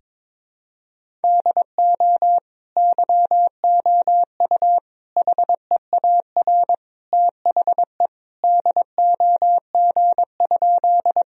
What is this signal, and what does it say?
Morse code, heard as a keyed tone, about 21 wpm, 700 Hz. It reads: DO YOU HEAR THE DOG?
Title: Sentences from Top 1000 Words - Review Part 2 - 22wpm